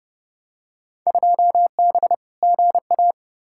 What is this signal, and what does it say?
Morse code, heard as a keyed tone, about 30 wpm, 700 Hz. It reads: 2B GA
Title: Field Day Exchanges Encore - 30wpm